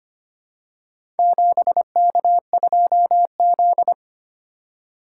Morse code, heard as a keyed tone, about 25 wpm, 700 Hz. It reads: 7K2Z
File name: practice / 25wpm / Four Letter-Number Combinations - 25wpm